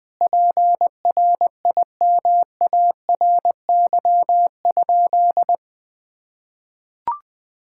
Morse code, heard as a keyed tone, about 20 wpm, 700 Hz. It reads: PRIMARY?
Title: Sentences from Top 1000 Words - Review Part 5 - 20wpm